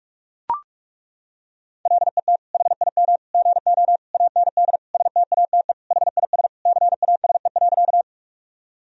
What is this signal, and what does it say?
Morse code, heard as a keyed tone, about 41 wpm, 700 Hz. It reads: LET HIM GO AND STATE HIS CASE.
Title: Sentences from Top 1000 Words - Review Part 2 - 45wpm